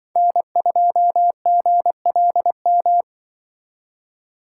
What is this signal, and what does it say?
Morse code, heard as a keyed tone, about 25 wpm, 700 Hz. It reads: N2GLM
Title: Speed Racing - USA 1x3 Call Signs 20wpm